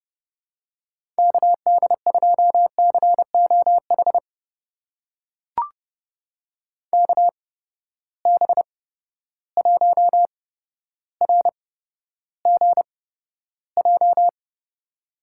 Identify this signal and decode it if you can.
Morse code, heard as a keyed tone, about 31 wpm, spaced as for 13 wpm, 700 Hz. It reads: KD2COH KB1RGJ
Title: USA Call Signs - 6 Characters - 8x Character Spacing - 30wpm